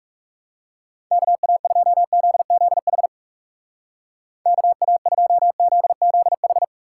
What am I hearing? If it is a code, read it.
Morse code, heard as a keyed tone, about 38 wpm, 700 Hz. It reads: KA2ZZH KA2ZZH